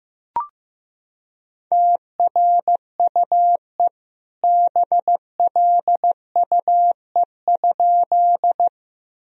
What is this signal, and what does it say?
Morse code, heard as a keyed tone, about 15 wpm, 700 Hz. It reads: TRUE BLUE?
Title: Sentences from Top 1000 Words - Review Part 3 - 15wpm